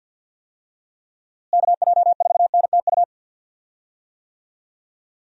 Morse code, heard as a keyed tone, about 46 wpm, 700 Hz. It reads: KJ4NTU
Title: USA Call Signs - 6 Characters - Repeated 3x - 50wpm